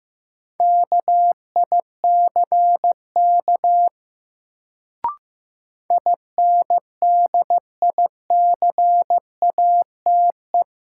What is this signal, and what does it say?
Morse code, heard as a keyed tone, about 16 wpm, 700 Hz. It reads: KICK INDICATE
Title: Sentences from Top 1300 Words - New Word Drill Alt - 15wpm